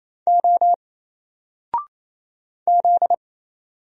Morse code, heard as a keyed tone, about 27 wpm, 700 Hz. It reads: O Z
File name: Single-Letter-Number-28wpm